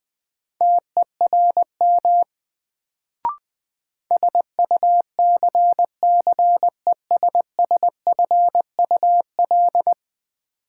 Morse code, heard as a keyed tone, about 20 wpm, 700 Hz. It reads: TERM SUCCESSFUL
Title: Sentences from Top 900 Words - New Word Drill - 20wpm